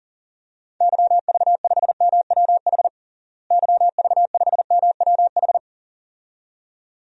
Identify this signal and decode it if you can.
Morse code, heard as a keyed tone, about 40 wpm, 700 Hz. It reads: YV5MWH YV5MWH
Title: DXCC-100 Call Signs - 6 Characters - Repeated 2x - 40wpm